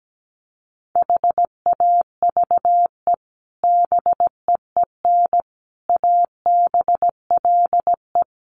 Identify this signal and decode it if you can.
Morse code, heard as a keyed tone, about 17 wpm, 700 Hz. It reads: HAVE BEEN ABLE
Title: Sets of 4 Words 17wpm